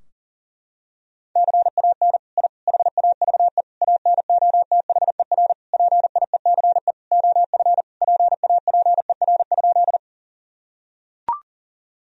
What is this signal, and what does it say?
Morse code, heard as a keyed tone, about 40 wpm, 700 Hz. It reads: CAN I HAVE ANOTHER PIECE OF PAPER?
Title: Sentences from Top 1000 Words - Review Part 2 - 40wpm